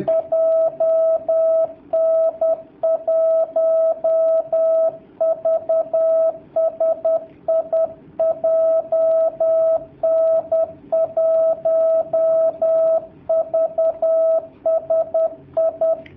Morse code, heard as a keyed tone, about 10 wpm, 650 Hz, with radio noise. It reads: JN1VSI JN1VSI
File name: jh8cbh / 2019cw / sp10- w